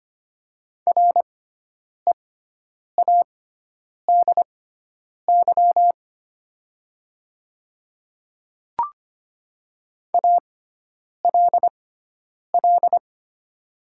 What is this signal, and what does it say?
Morse code, heard as a keyed tone, about 25 wpm, spaced as for 9 wpm, 700 Hz. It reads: READY ALL
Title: 6x Character Spacing - Top 500 Words - 25wpm